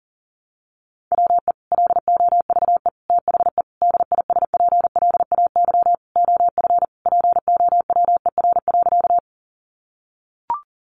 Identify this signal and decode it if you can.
Morse code, heard as a keyed tone, about 40 wpm, 700 Hz. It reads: WE LOVE THE DISPLAY OF POWER.
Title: Sentences from Top 1200 Words 40wpm